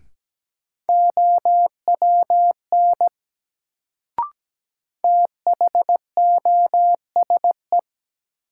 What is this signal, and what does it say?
Morse code, heard as a keyed tone, about 17 wpm, 700 Hz. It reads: OWN THOSE
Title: Sentences from Top 300 Words - New Word Drill - 17wpm